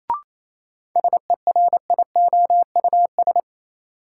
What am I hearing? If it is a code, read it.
Morse code, heard as a keyed tone, about 28 wpm, 700 Hz. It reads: SERIOUS
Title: Sentences from Top 1100 Words - New Word Drill Alt - 28wpm